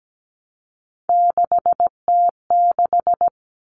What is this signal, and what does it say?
Morse code, heard as a keyed tone, about 17 wpm, 700 Hz. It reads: 6T6
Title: All Cut Numbers - 3-Digit 17wpm